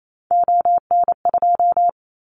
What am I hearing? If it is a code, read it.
Morse code, heard as a keyed tone, about 28 wpm, 700 Hz. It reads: ON2
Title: Callsign Prefixes for Top 50 DXCC - 28wpm